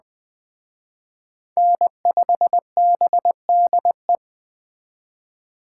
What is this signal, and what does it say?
Morse code, heard as a keyed tone, about 21 wpm, 700 Hz. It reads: N5BDE
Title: Speed Racing - USA 1x3 Call Signs 17wpm